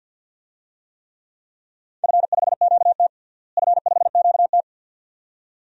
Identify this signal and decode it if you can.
Morse code, heard as a keyed tone, about 51 wpm, 700 Hz. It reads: F5QT F5QT
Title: DXCC-100 Call Signs - 4 Characters - Repeated 2x - 50wpm